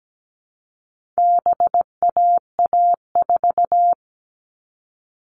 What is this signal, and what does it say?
Morse code, heard as a keyed tone, about 17 wpm, 700 Hz. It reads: BAA4
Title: All Cut Numbers - 4-Digit 17wpm